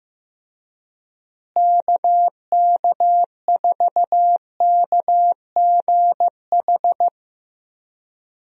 Morse code, heard as a keyed tone, about 15 wpm, 700 Hz. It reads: KK4KGH
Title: USA Call Signs - 6 Characters - Repeated 2x - 15wpm